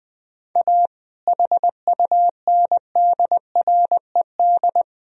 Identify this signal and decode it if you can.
Morse code, heard as a keyed tone, about 20 wpm, 700 Hz. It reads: A HUNDRED
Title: Sentences from Top 1000 Words - Review Part 2 - 20wpm